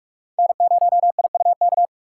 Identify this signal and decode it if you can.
Morse code, heard as a keyed tone, about 46 wpm, 700 Hz. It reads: N0IUK
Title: USA Call Signs - 4x Character Spacing - 45wpm